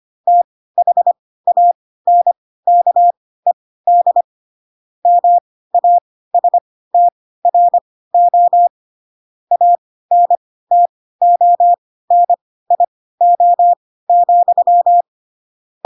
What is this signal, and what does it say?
Morse code, heard as a keyed tone, about 25 wpm, spaced as for 16 wpm, 700 Hz. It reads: THANKED MASTRO ANTONIO,
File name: Pinocchio / Pinocchio ch2 10 25 ik1hge